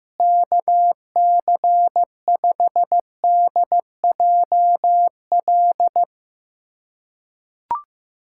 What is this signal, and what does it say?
Morse code, heard as a keyed tone, about 15 wpm, 700 Hz. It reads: KC5DJL
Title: USA Callsign DE Callsign 15wpm